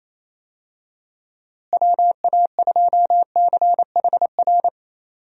Morse code, heard as a keyed tone, about 28 wpm, 700 Hz. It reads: WA2CHR
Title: USA Call Signs - 6 Characters - Repeated 2x - 28wpm